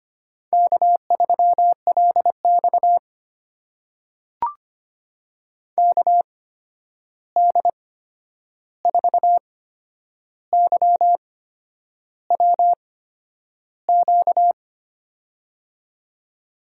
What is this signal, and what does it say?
Morse code, heard as a keyed tone, about 25 wpm, spaced as for 9 wpm, 700 Hz. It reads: K3LX KD4YWQ
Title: USA Call Signs - 8x Character Spacing - 25wpm